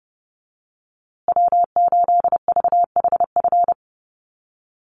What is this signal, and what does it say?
Morse code, heard as a keyed tone, about 30 wpm, 700 Hz. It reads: W8VHF
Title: USA Call Signs Repeated 2x 30wpm